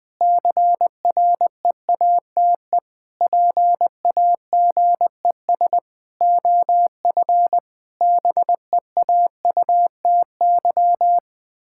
Morse code, heard as a keyed tone, about 20 wpm, 700 Hz. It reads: CREATE PAGES OF BEAUTY
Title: Pangram Challenge v2 - 20wpm